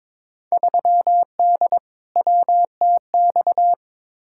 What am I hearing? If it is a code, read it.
Morse code, heard as a keyed tone, about 22 wpm, 700 Hz. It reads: 3D WTX
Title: Field Day Exchanges Encore - 22wpm